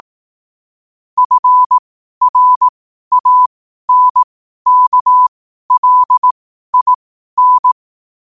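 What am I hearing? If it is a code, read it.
Morse code, heard as a keyed tone, about 18 wpm, spaced as for 12 wpm, 1000 Hz. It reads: FRANKLIN
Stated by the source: Morse code sent at 12 words per minute.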